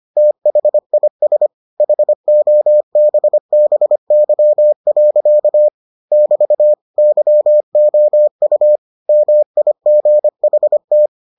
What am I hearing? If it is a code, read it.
Morse code, heard as a keyed tone, about 25 wpm, 600 Hz. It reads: THIS HOBBY. <BT> YOU MIGHT